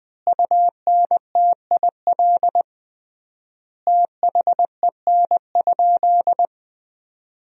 Morse code, heard as a keyed tone, about 20 wpm, 700 Hz. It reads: UNTIL THEN?
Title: 3x Word-Spacing - Sentences from Top 500 Words - 20wpm